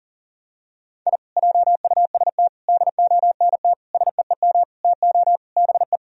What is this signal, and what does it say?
Morse code, heard as a keyed tone, about 40 wpm, 700 Hz. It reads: I JUST DONT SEEM TO BE
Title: Sentences from Top 1000 Words - Review Part 2 - 40wpm